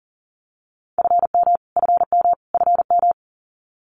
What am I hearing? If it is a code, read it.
Morse code, heard as a keyed tone, about 40 wpm, 700 Hz. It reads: FM FM FM
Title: Abbreviations Repeated 3x - 40wpm